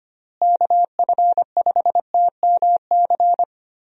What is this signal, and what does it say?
Morse code, heard as a keyed tone, about 25 wpm, 700 Hz. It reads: KF5TMC
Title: USA Call Signs - 6 Characters - Repeated 3x - 25wpm